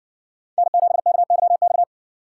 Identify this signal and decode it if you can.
Morse code, heard as a keyed tone, about 57 wpm, 700 Hz. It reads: N7KYX